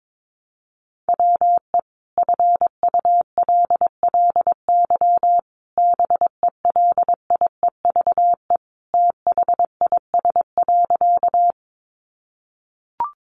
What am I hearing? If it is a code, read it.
Morse code, heard as a keyed tone, about 22 wpm, 700 Hz. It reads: WE FULLY BELIEVE THIS.
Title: Sentences from Top 1300 Words 22wpm